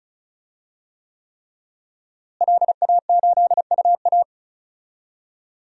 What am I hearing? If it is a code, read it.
Morse code, heard as a keyed tone, about 36 wpm, 700 Hz. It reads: LA8UA
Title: DXCC-100 Call Signs - 5 Characters - Repeated 3x - 35wpm